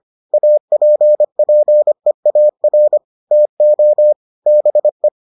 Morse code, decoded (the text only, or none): APPEAR TO BE